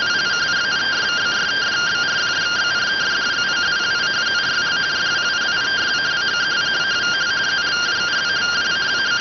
Two-tone teleprinter signal, about 170 Hz shift.